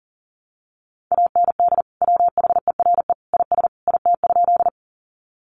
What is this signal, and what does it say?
Morse code, heard as a keyed tone, about 40 wpm, 700 Hz. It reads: AND WHERE IS IT?
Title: Sentences from Top 200 Words 40wpm